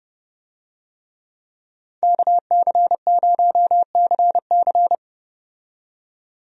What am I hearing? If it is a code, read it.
Morse code, heard as a keyed tone, about 32 wpm, 700 Hz. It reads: KC0CC
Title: USA Call Signs - 5 Characters - 30wpm